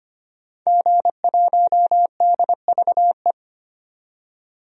Morse code, heard as a keyed tone, about 25 wpm, 700 Hz. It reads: G1DVE
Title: DXCC-100 Call Signs - 5 Characters - 25wpm